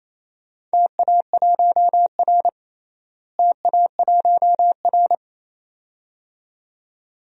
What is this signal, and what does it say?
Morse code, heard as a keyed tone, about 28 wpm, 700 Hz. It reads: TA1R TA1R
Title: DXCC-100 Call Signs - 4 Characters - Repeated 2x - 28wpm